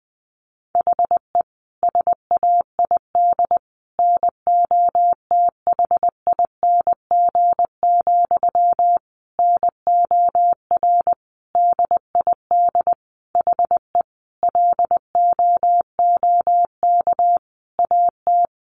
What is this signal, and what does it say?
Morse code, heard as a keyed tone, about 20 wpm, 700 Hz. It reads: HE SAID NOTHING, NOR DID HE LOOK AT
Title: Sentences from Top 1500 Words 20wpm